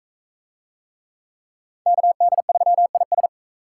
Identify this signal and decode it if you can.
Morse code, heard as a keyed tone, about 41 wpm, 700 Hz. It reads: KD3IS